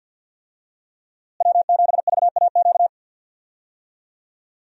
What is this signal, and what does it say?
Morse code, heard as a keyed tone, about 51 wpm, 700 Hz. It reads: W7VAQ
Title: USA Call Signs - 5 Characters - Repeated 2x - 50wpm